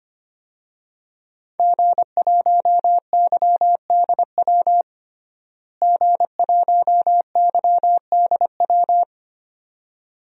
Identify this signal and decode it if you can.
Morse code, heard as a keyed tone, about 25 wpm, 700 Hz. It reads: G1YDW G1YDW
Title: DXCC-100 Call Signs - 5 Characters - Repeated 2x - 25wpm